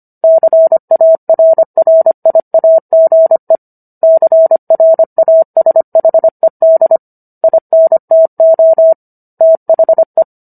Morse code, heard as a keyed tone, about 25 wpm, 650 Hz. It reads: CARRIAGE CRASHED INTO THE